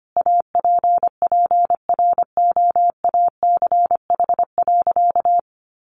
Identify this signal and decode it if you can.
Morse code, heard as a keyed tone, about 25 wpm, 700 Hz. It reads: APPROACH.
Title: Sentences from Top 1400 Words 25wpm